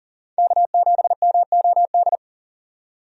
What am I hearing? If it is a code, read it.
Morse code, heard as a keyed tone, about 40 wpm, 700 Hz. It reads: K7MOD
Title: USA Call Signs - 5 Characters - 40wpm